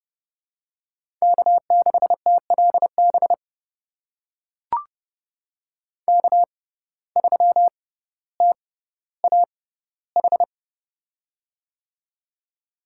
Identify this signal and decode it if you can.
Morse code, heard as a keyed tone, about 31 wpm, spaced as for 15 wpm, 700 Hz. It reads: K6TLB K3TAH
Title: USA Call Signs - 5 Characters - 6x Character Spacing - 30wpm